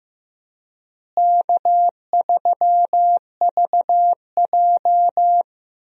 Morse code, heard as a keyed tone, about 16 wpm, 700 Hz. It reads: K3VJ
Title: USA Call Signs - 4 Characters - Repeated 3x - 15wpm